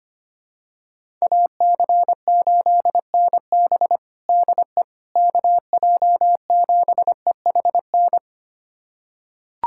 Morse code, heard as a keyed tone, about 25 wpm, 700 Hz. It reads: AC8NB DE KJ7EHN
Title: USA Callsign DE Callsign 25wpm